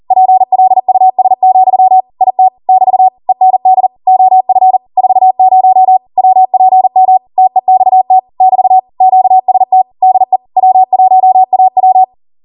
Morse code, heard as a keyed tone, about 40 wpm, 750 Hz. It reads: PLUS, IT <BT> END OF 40 WPM TEXT <BT> QST DE W1AW